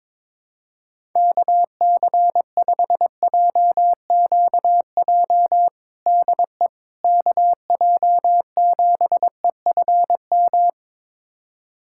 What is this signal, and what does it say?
Morse code, heard as a keyed tone, about 22 wpm, 700 Hz. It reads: KC5JQJ DE KJ7EFM
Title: USA Callsign DE Callsign - 6 Characters - 22wpm